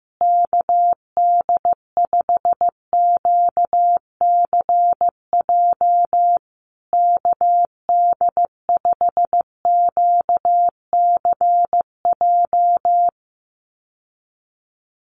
Morse code, heard as a keyed tone, about 15 wpm, 700 Hz. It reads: KD5QCJ KD5QCJ
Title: USA Call Signs Repeated 3x 15wpm